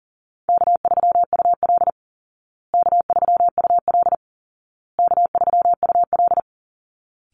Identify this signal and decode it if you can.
Morse code, heard as a keyed tone, about 40 wpm, 700 Hz. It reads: K3UL K3UL K3UL
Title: Easy Call Signs Repeated 3x - 40wpm